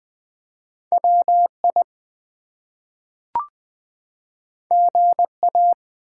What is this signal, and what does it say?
Morse code, heard as a keyed tone, about 20 wpm, 700 Hz. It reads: WI GA
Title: ARRL Sections v2 - 20wpm